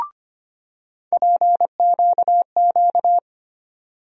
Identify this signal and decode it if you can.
Morse code, heard as a keyed tone, about 25 wpm, 700 Hz. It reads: PQQ
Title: USA Call Sign Postfixes - 3 Letters 25wpm